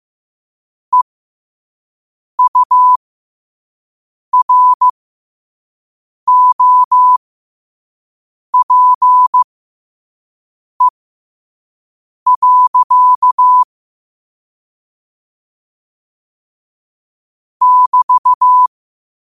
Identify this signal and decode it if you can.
Morse code, heard as a keyed tone, about 15 wpm, spaced as for 5 wpm, 1000 Hz. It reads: EUROPE. <BT>
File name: qotdfiles / 5wpm